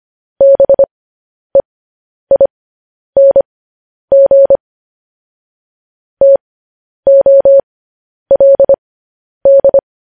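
Morse code, heard as a keyed tone, about 25 wpm, spaced as for 10 wpm, 550 Hz. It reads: BEING TOLD
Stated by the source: Morse code news headlines sent at 10 wpm for practice.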